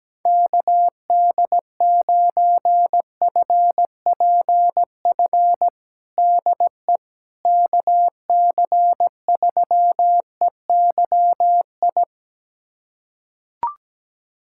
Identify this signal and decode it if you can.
Morse code, heard as a keyed tone, about 17 wpm, 700 Hz. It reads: KD9FPF DE KC3EYI
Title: USA Callsign DE Callsign - 6 Characters - 17wpm